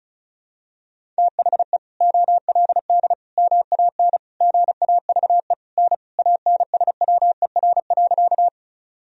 Morse code, heard as a keyed tone, about 36 wpm, 700 Hz. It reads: THE OLD MAN GAVE N ANSWER.
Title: Sentences from Top 500 Words and Abbreviations 35wpm